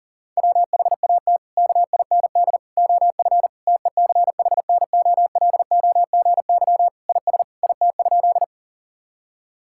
Morse code, heard as a keyed tone, about 40 wpm, 700 Hz. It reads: WHAT KIND OF TECHNOLOGY IS IT?
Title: Sentences from Top 1000 Words - Review Part 5 - 40wpm